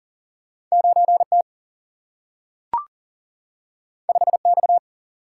Morse code, heard as a keyed tone, about 38 wpm, 700 Hz. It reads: 9 T 5 X
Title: Single-Letter Single-Number 40wpm